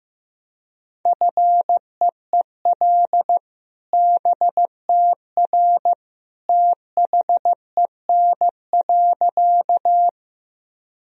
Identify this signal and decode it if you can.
Morse code, heard as a keyed tone, about 15 wpm, 700 Hz. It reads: FEEL BTR THEN.
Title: Sentences from Top 400 Words and Abbreviations 15wpm